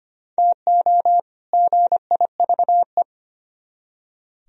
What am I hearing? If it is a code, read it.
Morse code, heard as a keyed tone, about 32 wpm, 700 Hz. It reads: TO GIVE